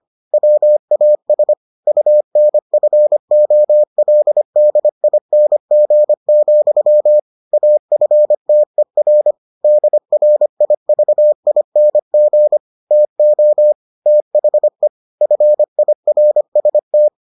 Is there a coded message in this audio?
Yes: Morse code from 0.3 to 17.1 s, heard as a keyed tone, about 25 wpm, 600 Hz. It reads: WAS UNFOLDING, AFTER DRIVING TO THE FIRST